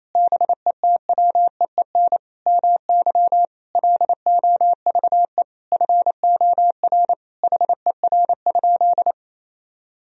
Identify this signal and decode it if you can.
Morse code, heard as a keyed tone, about 28 wpm, 700 Hz. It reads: BETWEEN MY LOVE FOR HER?
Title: Sentences from Top 1000 Words - Review Part 1 - 28wpm